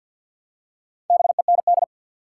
Morse code, heard as a keyed tone, about 50 wpm, 700 Hz. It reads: BEND
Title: Four Letter Words - 50wpm